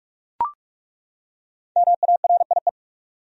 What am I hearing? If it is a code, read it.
Morse code, heard as a keyed tone, about 41 wpm, 700 Hz. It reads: MARIE
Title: USA Common Names - Max 5 Letters - 45wpm